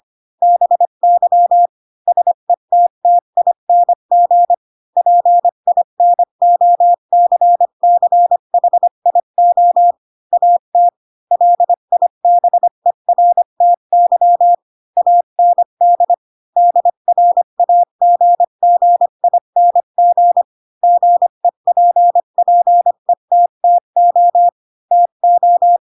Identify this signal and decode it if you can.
Morse code, heard as a keyed tone, about 25 wpm, 700 Hz. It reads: BY SETTING PINOCCHIO AT LIBERTY AND DRAGGING GEPPETTO TO
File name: Pinocchio / Pinocchio ch3 20 25 ik1hge